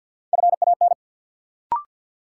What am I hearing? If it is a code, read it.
Morse code, heard as a keyed tone, about 53 wpm, 700 Hz. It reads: FAN
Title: Sentences from Top 1300 Words - New Word Drill Alt - 50wpm